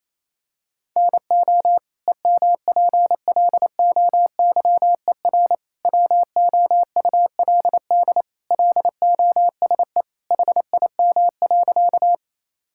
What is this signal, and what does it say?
Morse code, heard as a keyed tone, about 27 wpm, 700 Hz. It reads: NO EMPLOYER WOULD LOSE HIM.
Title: Sentences from Top 1600 Words 28wpm